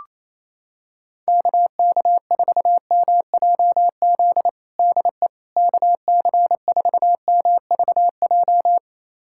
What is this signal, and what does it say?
Morse code, heard as a keyed tone, about 28 wpm, 700 Hz. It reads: KK4MJZ DE KC4MVJ
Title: USA Callsign DE Callsign - 6 Characters - 28wpm